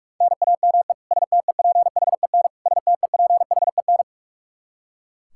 Morse code, heard as a keyed tone, about 41 wpm, 700 Hz. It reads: NAME STEPHEN STEPHEN
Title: Practice QSO - Exchange 1 - Beginning - 45wpm